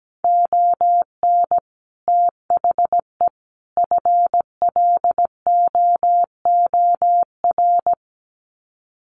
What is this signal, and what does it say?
Morse code, heard as a keyed tone, about 17 wpm, 700 Hz. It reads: ON THE FLOOR
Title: Sets of 3 Words 17wpm